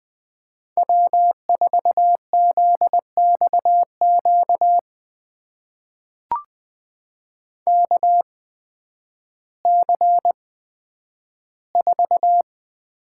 Morse code, heard as a keyed tone, about 20 wpm, spaced as for 11 wpm, 700 Hz. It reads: W4ZXQ KC4
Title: USA Call Signs - 5 Characters - 8x Character Spacing - 20wpm